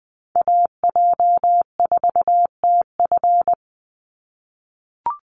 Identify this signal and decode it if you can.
Morse code, heard as a keyed tone, about 20 wpm, 700 Hz. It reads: AJ4TF
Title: USA Call Signs 20wpm